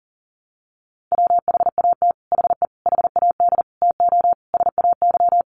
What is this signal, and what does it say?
Morse code, heard as a keyed tone, about 40 wpm, 700 Hz. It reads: WHAT HE HAD TO SAY
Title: Sets of 5 Words Encore2 40wpm